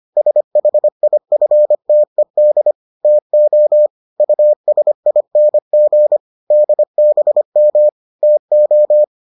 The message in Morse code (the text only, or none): SHIFTED TO USING DBM TO